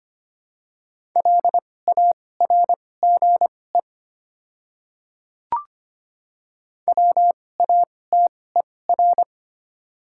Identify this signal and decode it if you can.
Morse code, heard as a keyed tone, about 25 wpm, spaced as for 18 wpm, 700 Hz. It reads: LARGE WATER
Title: 2x Character Spacing - Top 200 Words - 25wpm